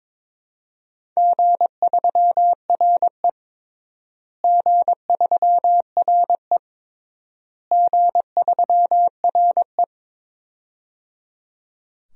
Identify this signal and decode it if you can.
Morse code, heard as a keyed tone, about 22 wpm, 700 Hz. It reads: G3RE G3RE G3RE
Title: DXCC-100 Call Signs - 4 Characters - Repeated 3x - 22wpm